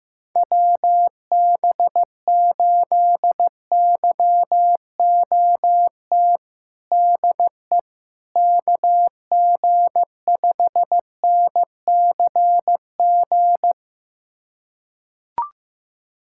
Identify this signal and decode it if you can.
Morse code, heard as a keyed tone, about 15 wpm, 700 Hz. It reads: WB8YOT DE KG5NCG
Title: USA Callsign DE Callsign 15wpm